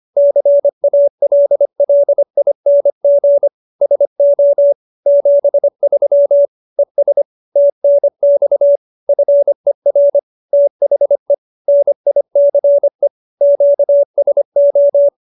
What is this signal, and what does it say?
Morse code, heard as a keyed tone, about 25 wpm, 570 Hz. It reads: CALLING SO 73 ES TNX FER THE NICE QSO